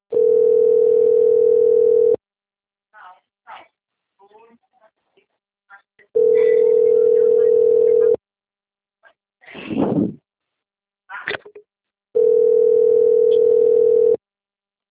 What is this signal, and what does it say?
Ringback tone (North America), 440 + 480 Hz, with speech.
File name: rg-1600-unknown- - - .36816